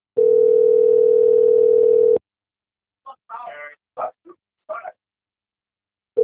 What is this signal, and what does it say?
Ringback tone (North America), 440 + 480 Hz.